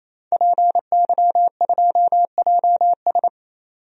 Morse code, heard as a keyed tone, about 28 wpm, 700 Hz. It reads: PY2JS
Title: DXCC-100 Call Signs - 5 Characters - 28wpm